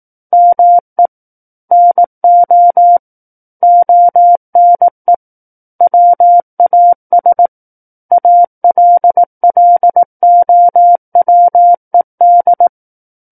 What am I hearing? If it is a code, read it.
Morse code, heard as a keyed tone, about 18 wpm, 700 Hz. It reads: ME NO ONE WAS ALLOWED
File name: 18 WPM